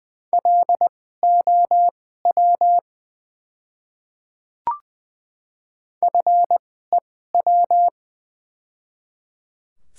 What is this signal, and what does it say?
Morse code, heard as a keyed tone, about 20 wpm, spaced as for 14 wpm, 700 Hz. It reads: LOW FEW
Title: 2x Character Spacing - Top 300 Words - 20wpm